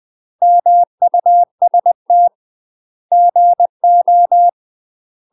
Morse code, heard as a keyed tone, about 20 wpm, 700 Hz. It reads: MUST GO
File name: UK MASTER QSOs TRAINING / 20WPM / UK MASTER QSOs 0004